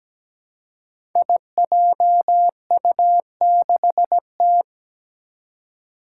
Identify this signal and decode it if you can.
Morse code, heard as a keyed tone, about 17 wpm, 700 Hz. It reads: IJU6T
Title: Five Challenging Letter-Number Combinations 17wpm